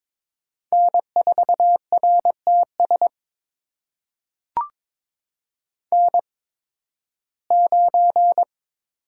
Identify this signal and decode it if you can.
Morse code, heard as a keyed tone, about 22 wpm, spaced as for 14 wpm, 700 Hz. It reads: N4RTS N9
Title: USA Call Signs - 5 Characters - 8x Character Spacing - 22wpm